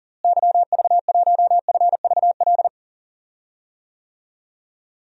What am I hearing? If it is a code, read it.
Morse code, heard as a keyed tone, about 40 wpm, 700 Hz. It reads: YV1FVL
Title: DXCC-100 Call Signs - 6 Characters - Repeated 2x - 40wpm